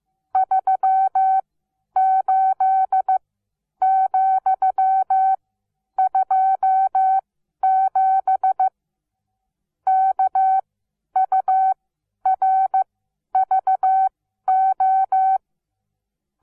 Morse code, heard as a keyed tone, about 15 wpm, spaced as for 10 wpm, 760 Hz. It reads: 38,27 KURVO
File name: Onlinelehrgang / morsen / DJ4UF Morsen17